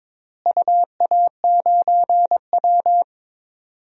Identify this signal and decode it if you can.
Morse code, heard as a keyed tone, about 22 wpm, 700 Hz. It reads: UA9W
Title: DXCC-100 Call Signs - 4 Characters - Repeated 2x - 22wpm